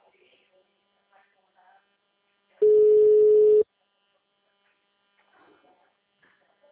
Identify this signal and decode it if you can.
Ringback tone (Europe, 425 Hz), with speech.